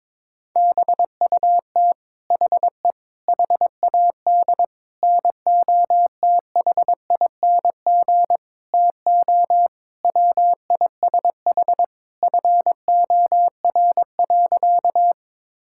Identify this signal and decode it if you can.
Morse code, heard as a keyed tone, about 22 wpm, 700 Hz. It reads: BUT HE HAD NOTHING TO WISH FOR.
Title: Sentences from Top 1600 Words 22wpm